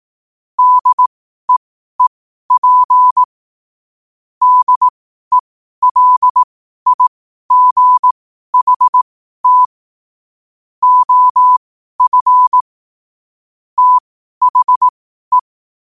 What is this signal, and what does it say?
Morse code, heard as a keyed tone, about 18 wpm, spaced as for 12 wpm, 1000 Hz. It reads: DEEP DELIGHT OF THE
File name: qotdfiles / 12wpm